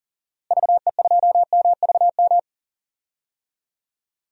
Morse code, heard as a keyed tone, about 40 wpm, 700 Hz. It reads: VE2MVM
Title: DXCC-100 Call Signs - 6 Characters - Repeated 2x - 40wpm